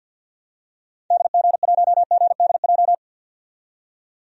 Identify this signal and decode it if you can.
Morse code, heard as a keyed tone, about 50 wpm, 700 Hz. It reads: DG1GDJ
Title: DXCC-100 Call Signs - 6 Characters - 50wpm